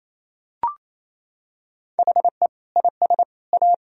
Morse code, heard as a keyed tone, about 28 wpm, 700 Hz. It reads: HE IS A
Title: Sentences from Top 1000 Words - Review Part 2 - 28wpm